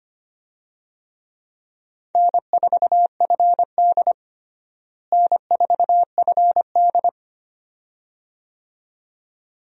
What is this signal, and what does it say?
Morse code, heard as a keyed tone, about 25 wpm, 700 Hz. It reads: N4FD N4FD
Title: USA Call Signs - 4 Characters - Repeated 2x - 25wpm